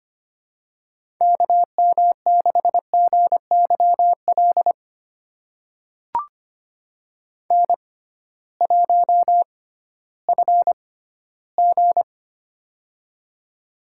Morse code, heard as a keyed tone, about 25 wpm, spaced as for 15 wpm, 700 Hz. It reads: KM6GYL N1FG
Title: USA Call Signs - 6x Character Spacing - 25wpm